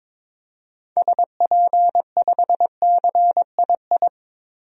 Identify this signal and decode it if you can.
Morse code, heard as a keyed tone, about 22 wpm, 700 Hz. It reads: SP5CII
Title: DXCC-100 Call Signs - 6 Characters - 22wpm